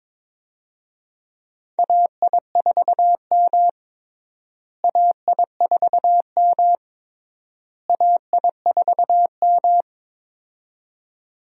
Morse code, heard as a keyed tone, about 22 wpm, 700 Hz. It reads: AI4M AI4M AI4M
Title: USA Call Signs - 4 Characters - Repeated 3x - 22wpm